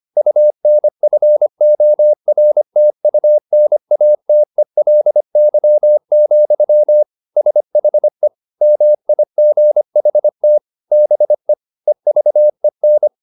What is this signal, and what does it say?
Morse code, heard as a keyed tone, about 25 wpm, 600 Hz. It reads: UNFORTUNATELY, SHE MIGHT BE EVEN